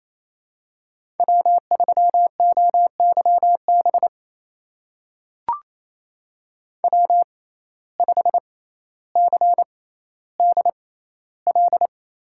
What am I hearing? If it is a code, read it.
Morse code, heard as a keyed tone, about 28 wpm, spaced as for 14 wpm, 700 Hz. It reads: W3OYB W5CDL
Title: USA Call Signs - 5 Characters - 6x Character Spacing - 28wpm